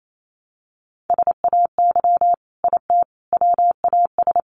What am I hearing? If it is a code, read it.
Morse code, heard as a keyed tone, about 28 wpm, 700 Hz. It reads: SAY IT WAS
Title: Sentences from Top 800 Words 28wpm